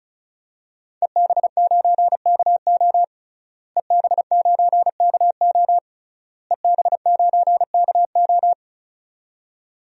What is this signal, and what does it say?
Morse code, heard as a keyed tone, about 34 wpm, 700 Hz. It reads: EB9KO EB9KO EB9KO
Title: DXCC-100 Call Signs - 5 Characters - Repeated 3x - 35wpm